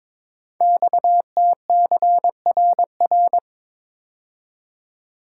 Morse code, heard as a keyed tone, about 22 wpm, 700 Hz. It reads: XTCRR
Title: Code Groups - 5 Letters - 22wpm